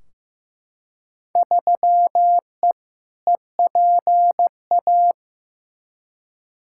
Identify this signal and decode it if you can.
Morse code, heard as a keyed tone, about 15 wpm, 700 Hz. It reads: 3E EPA
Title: Field Day Exchanges Encore - 15wpm